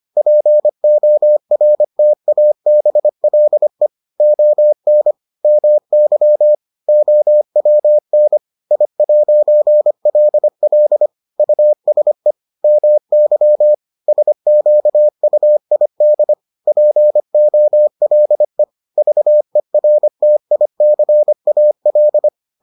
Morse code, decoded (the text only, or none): PORTABLE ON MY OWN I'LL USE MY SQUID POLE VERTICAL